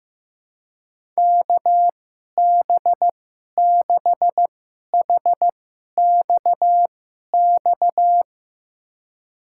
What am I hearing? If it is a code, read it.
Morse code, heard as a keyed tone, about 15 wpm, spaced as for 11 wpm, 700 Hz. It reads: KB6HXX
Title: USA Call Signs - 6 Characters - 2x Character Spacing - 15wpm